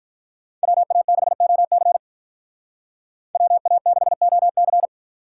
Morse code, heard as a keyed tone, about 43 wpm, 700 Hz. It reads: WA6OC WA6OC
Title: Speed Racing - USA 2x2 Call Signs 35wpm